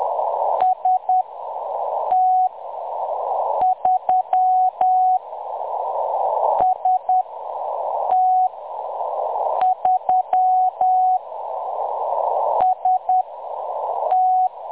Morse code, heard as a keyed tone, about 10 wpm, spaced as for 6 wpm, 740 Hz. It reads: ST3 ST3 ST
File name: M23-ST3- -0858utc-10184khz